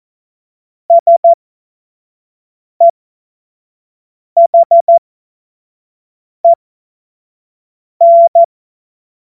Morse code, heard as a keyed tone, about 14 wpm, spaced as for 5 wpm, 680 Hz. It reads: SEHEN
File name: Morsekurs / Witze / Witz 19 März 2024 25 F70